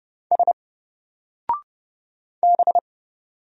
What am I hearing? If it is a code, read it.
Morse code, heard as a keyed tone, about 30 wpm, 700 Hz. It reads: S B
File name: Single-Letter-Number-30wpm